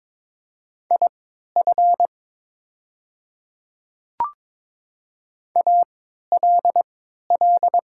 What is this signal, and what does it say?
Morse code, heard as a keyed tone, about 22 wpm, spaced as for 12 wpm, 700 Hz. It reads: IF ALL
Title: 3x Character Spacing - Top 100 Words - 22wpm